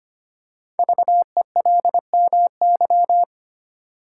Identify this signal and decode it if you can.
Morse code, heard as a keyed tone, about 25 wpm, 700 Hz. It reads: VELMY
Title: Code Groups - 5 Letters - 25wpm